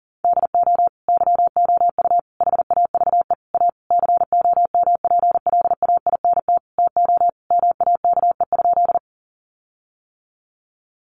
Morse code, heard as a keyed tone, about 40 wpm, 700 Hz. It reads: DO YOU HAVE A COMPLAINT TO MAKE?
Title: Sentences from Top 1400 Words 40wpm